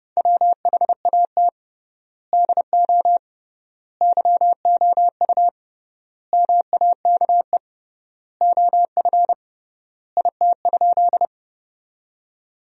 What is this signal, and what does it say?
Morse code, heard as a keyed tone, about 30 wpm, 700 Hz. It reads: WHAT DO YOU MAKE OF IT?
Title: 3x Word-Spacing - Sentences from Top 100 Words - 30wpm